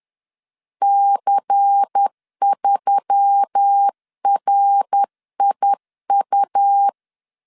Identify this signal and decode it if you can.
Morse code, heard as a keyed tone, about 10 wpm, 800 Hz. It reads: C3RIU
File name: CW / DET / 1 6 1500C S B 10wpm 4137